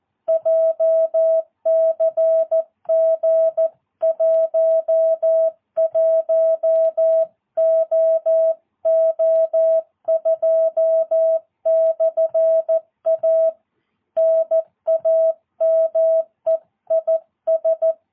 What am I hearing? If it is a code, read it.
Morse code, heard as a keyed tone, about 14 wpm, 650 Hz. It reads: JCG 11OO2/A NAME IS
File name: jh8cbh / 2022cw